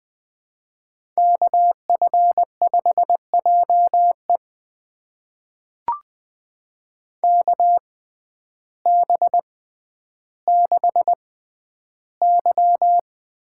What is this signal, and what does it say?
Morse code, heard as a keyed tone, about 20 wpm, spaced as for 11 wpm, 700 Hz. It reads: KF5JE KB6Y
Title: USA Call Signs - 6x Character Spacing - 20wpm